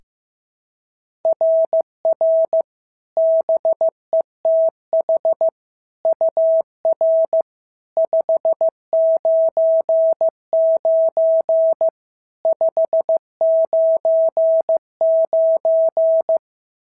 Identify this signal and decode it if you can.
Morse code, heard as a keyed tone, about 14 wpm, 650 Hz. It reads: RR BETH UR 599 599
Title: SOTA QSO Practice 15wpm